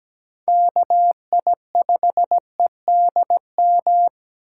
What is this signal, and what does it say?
Morse code, heard as a keyed tone, about 17 wpm, 700 Hz. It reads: KI5EDM
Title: USA Call Signs - 6 Characters - 8x Character Spacing - 17wpm